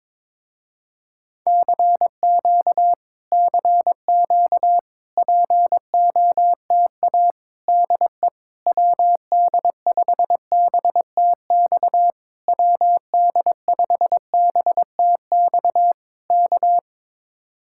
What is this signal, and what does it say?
Morse code, heard as a keyed tone, about 22 wpm, 700 Hz. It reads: CQ CQ POTA DE WD5BTX WD5BTX K
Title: Speed-Racing Common QSO Phrases - 22wpm